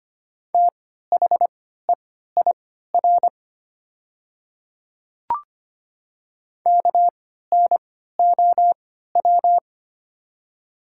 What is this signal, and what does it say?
Morse code, heard as a keyed tone, about 25 wpm, spaced as for 14 wpm, 700 Hz. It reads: THEIR KNOW
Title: 3x Character Spacing - Top 100 Words - 25wpm